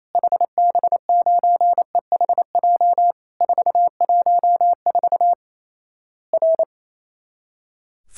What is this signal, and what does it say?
Morse code, heard as a keyed tone, about 28 wpm, 700 Hz. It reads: HB9EHJ 414 R
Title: Worked All Europe DX Contest - QTC Exchange - 28wpm